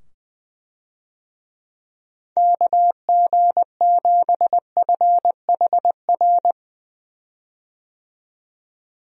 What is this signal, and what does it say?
Morse code, heard as a keyed tone, about 21 wpm, 700 Hz. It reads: KG7FHR
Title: USA Call Signs - 6 Characters - 20wpm